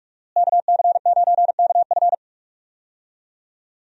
Morse code, heard as a keyed tone, about 42 wpm, 700 Hz. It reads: KC9KF
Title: USA Call Signs - 5 Characters - Repeated 2x - 45wpm